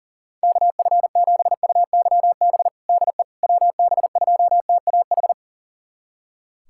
Morse code, heard as a keyed tone, about 37 wpm, 700 Hz. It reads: KF7UYB DE WB2TAH
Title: USA Callsign DE Callsign - 6 Characters - 40wpm